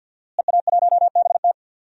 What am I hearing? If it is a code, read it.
Morse code, heard as a keyed tone, about 52 wpm, 700 Hz. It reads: EA1BT
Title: DXCC-100 Call Signs - 5 Characters - Repeated 2x - 50wpm